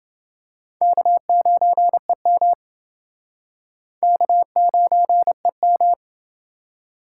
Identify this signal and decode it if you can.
Morse code, heard as a keyed tone, about 28 wpm, 700 Hz. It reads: K9EM K9EM